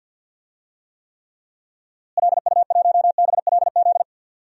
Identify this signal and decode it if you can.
Morse code, heard as a keyed tone, about 53 wpm, 700 Hz. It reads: LU1BLZ
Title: DXCC-100 Call Signs - 6 Characters - Repeated 3x - 50wpm